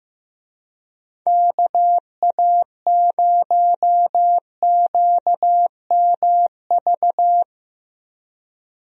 Morse code, heard as a keyed tone, about 15 wpm, 700 Hz. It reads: KA0QMV
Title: USA Call Signs - 6 Characters - Repeated 3x - 15wpm